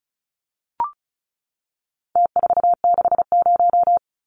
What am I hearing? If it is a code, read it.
Morse code, heard as a keyed tone, about 34 wpm, 700 Hz. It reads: T460
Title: Numbers - Contest Style - 4-Digit 35wpm